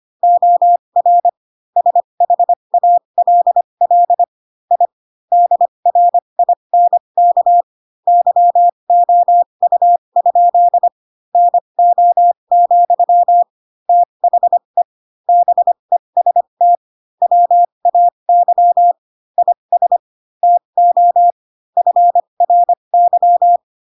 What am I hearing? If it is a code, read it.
Morse code, heard as a keyed tone, about 25 wpm, spaced as for 22 wpm, 700 Hz. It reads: OR SHALL I DRINK YOU? NO, THE BEST WAY IS TO FRY